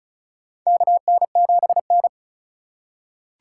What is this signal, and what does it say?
Morse code, heard as a keyed tone, about 34 wpm, 700 Hz. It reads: KN7N
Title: USA Call Signs - 4 Characters - 3x Character Spacing - 35wpm